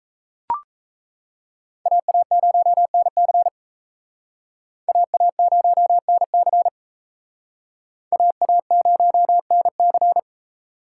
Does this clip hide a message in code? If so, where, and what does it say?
Yes: Morse code from 1.9 to 10.2 s, heard as a keyed tone, about 35 wpm, 700 Hz. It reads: AA0NC AA0NC AA0NC